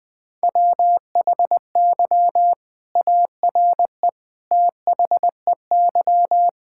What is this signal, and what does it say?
Morse code, heard as a keyed tone, about 20 wpm, 700 Hz. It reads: WHY ARE THEY
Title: Sentences from Top 1000 Words - Review Part 6 - 20wpm